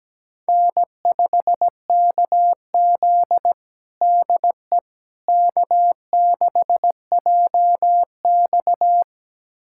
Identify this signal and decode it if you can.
Morse code, heard as a keyed tone, about 17 wpm, 700 Hz. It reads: N5KZ DE K6JX
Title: USA Callsign DE Callsign - 4 Characters - 17wpm